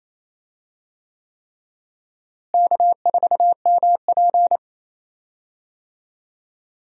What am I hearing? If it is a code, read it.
Morse code, heard as a keyed tone, about 28 wpm, 700 Hz. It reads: K4MP
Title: USA Call Signs - 4 Characters - Repeated 3x - 28wpm